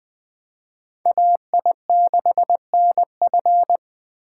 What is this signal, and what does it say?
Morse code, heard as a keyed tone, about 23 wpm, 700 Hz. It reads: AI6NF